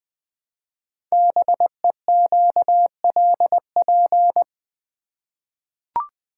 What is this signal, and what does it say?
Morse code, heard as a keyed tone, about 20 wpm, 700 Hz. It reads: BEQLP
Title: Code Groups - 5 Letters - 20wpm